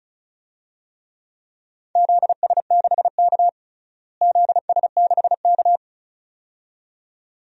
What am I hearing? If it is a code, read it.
Morse code, heard as a keyed tone, about 34 wpm, 700 Hz. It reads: ZS6K ZS6K
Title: DXCC-100 Call Signs - 4 Characters - Repeated 2x - 35wpm